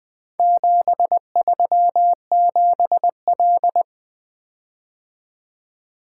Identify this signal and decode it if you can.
Morse code, heard as a keyed tone, about 23 wpm, 700 Hz. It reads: 737L
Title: Speed-Racing - Four Letter-Number Combinations - 20wpm